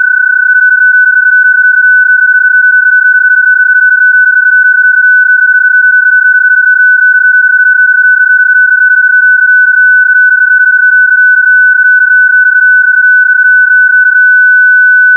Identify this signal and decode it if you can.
PSK31, about 31.25 baud.